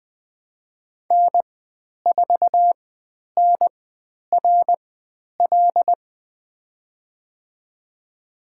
Morse code, heard as a keyed tone, about 20 wpm, spaced as for 10 wpm, 700 Hz. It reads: N4NRL
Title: 109 Lesson 18g - Letter m - Call Signs 20wpm